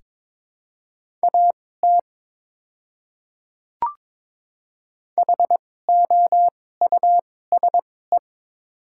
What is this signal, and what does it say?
Morse code, heard as a keyed tone, about 22 wpm, spaced as for 16 wpm, 700 Hz. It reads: AT HOUSE
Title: 2x Character Spacing - Top 200 Words - 22wpm